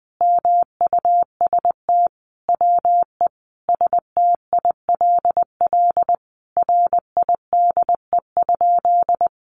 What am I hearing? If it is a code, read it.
Morse code, heard as a keyed tone, about 20 wpm, 700 Hz. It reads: MUST WE STILL RIDE?
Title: Sentences from Top 1200 Words 20wpm